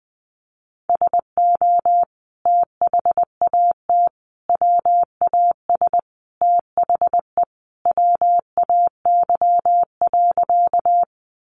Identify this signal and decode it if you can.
Morse code, heard as a keyed tone, about 20 wpm, 700 Hz. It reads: SO THAT WAS THE WAY.
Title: Sentences from Top 100 Words 20wpm